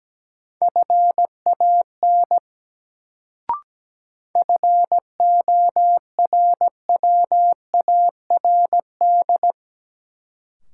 Morse code, heard as a keyed tone, about 17 wpm, 700 Hz. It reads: FAN FORWARD
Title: Sentences from Top 1300 Words - New Word Drill Alt - 17wpm